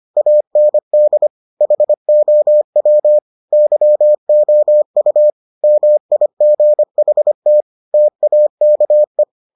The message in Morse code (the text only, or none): AND HOW YOU MIGHT TAKE